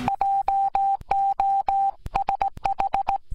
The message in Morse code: JOSH